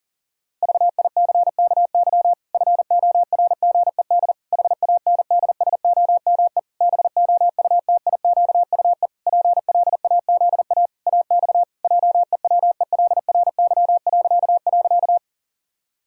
Morse code, heard as a keyed tone, about 37 wpm, 700 Hz. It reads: VICKY FORGED HANDSOME BOUTIQUE PLAZA AX JEWELRY..
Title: Pangram Challenge v2 - 40wpm